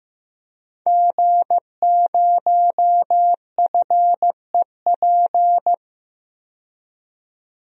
Morse code, heard as a keyed tone, about 15 wpm, 700 Hz. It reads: G0FEP
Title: DXCC-100 Call Signs - 5 Characters - 15wpm